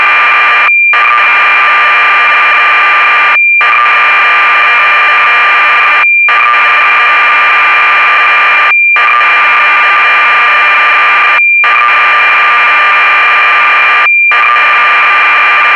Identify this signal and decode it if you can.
Cassette data (Kansas City type).